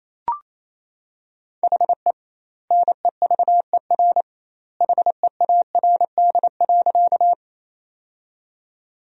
Morse code, heard as a keyed tone, about 27 wpm, 700 Hz. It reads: HE NEVER HEARD.
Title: 2x Word-Spacing - Sentences from Top 400 Words - 28wpm